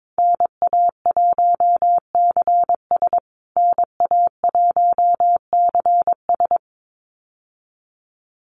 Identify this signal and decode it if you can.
Morse code, heard as a keyed tone, about 22 wpm, 700 Hz. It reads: NA1CS NA1CS
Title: USA Call Signs Repeated 2x 22wpm